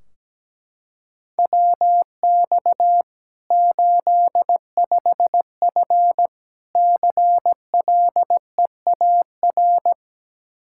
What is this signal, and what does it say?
Morse code, heard as a keyed tone, about 17 wpm, 700 Hz. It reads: WX 85F CLEAR
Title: Practice QSO - Exchange 2 - Info - 17wpm